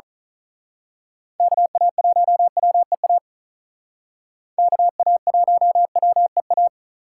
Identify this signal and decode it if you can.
Morse code, heard as a keyed tone, about 42 wpm, 700 Hz. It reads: KA1WEA KA1WEA
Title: Speed Racing - USA 2x3 Call Signs 35wpm